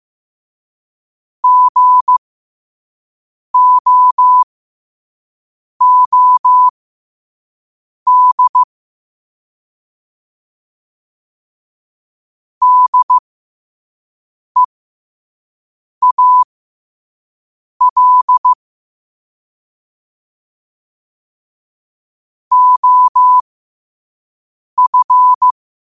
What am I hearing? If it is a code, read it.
Morse code, heard as a keyed tone, about 15 wpm, spaced as for 5 wpm, 1000 Hz. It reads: GOOD DEAL OF
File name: 5wpm